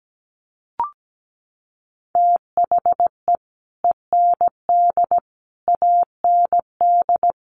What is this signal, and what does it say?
Morse code, heard as a keyed tone, about 17 wpm, 700 Hz. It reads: THE END AND
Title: Sentences from Top 1400 Words 17wpm